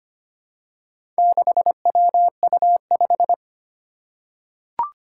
Morse code, heard as a keyed tone, about 25 wpm, 700 Hz. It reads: 6WU5
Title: Four Challenging Letter-Number Combinations 25wpm